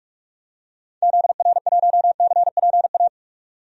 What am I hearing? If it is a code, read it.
Morse code, heard as a keyed tone, about 44 wpm, 700 Hz. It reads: ZR1CPA
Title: DXCC-100 Call Signs - 6 Characters - 45wpm